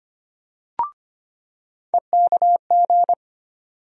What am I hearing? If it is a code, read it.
Morse code, heard as a keyed tone, about 25 wpm, 700 Hz. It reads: EKG
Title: USA Call Sign Postfixes - 3 Letters 25wpm